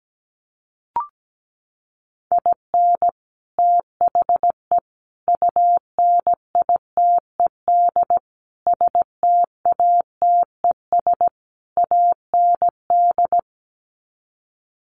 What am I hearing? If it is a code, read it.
Morse code, heard as a keyed tone, about 17 wpm, 700 Hz. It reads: IN THE UNITED STATES AND
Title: Sets of 5 Words 17wpm